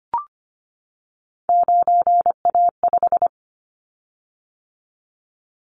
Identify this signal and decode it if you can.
Morse code, heard as a keyed tone, about 25 wpm, 700 Hz. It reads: 9A5
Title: Callsign Prefixes for Top 25 DXCC - 25wpm